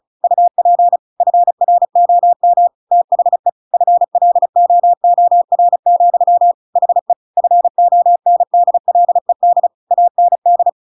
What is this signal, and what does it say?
Morse code, heard as a keyed tone, about 34 wpm, 700 Hz. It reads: UP FROM THE FLOOR, HE FONDLED AND
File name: Pinocchio / Pinocchio ch7 35 35 ik1hge